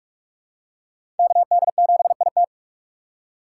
Morse code, heard as a keyed tone, about 47 wpm, 700 Hz. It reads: KD7IT
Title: USA Call Signs - 5 Characters - 6x Character Spacing - 45wpm